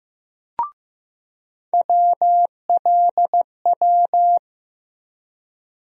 Morse code, heard as a keyed tone, about 15 wpm, 700 Hz. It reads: WLW
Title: USA Call Sign Postfixes - 3 Letters 15wpm